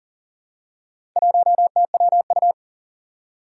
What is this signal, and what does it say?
Morse code, heard as a keyed tone, about 38 wpm, 700 Hz. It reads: 1TWU
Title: Four Challenging Letter-Number Combinations 40wpm